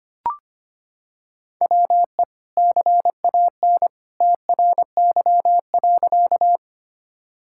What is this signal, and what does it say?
Morse code, heard as a keyed tone, about 25 wpm, 700 Hz. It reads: WE CAN TRY.
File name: practice / 25wpm / Sentences with 2-Word Phrases - 25wpm